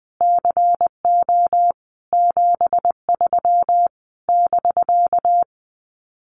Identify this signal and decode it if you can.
Morse code, heard as a keyed tone, about 20 wpm, 700 Hz. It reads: CO 73 <BK>
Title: Common QSO Phrases Encore 20wpm